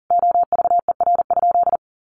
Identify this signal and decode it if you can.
Morse code, heard as a keyed tone, about 40 wpm, 700 Hz. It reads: OVER?
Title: Sentences from Top 1500 Words 40wpm